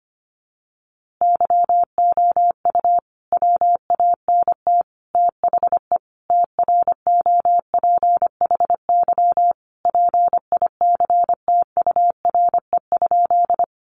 Morse code, heard as a keyed tone, about 25 wpm, 700 Hz. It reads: YOU WANT THE TROPHY PICTURE?
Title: A Sound Of Thunder 25wpm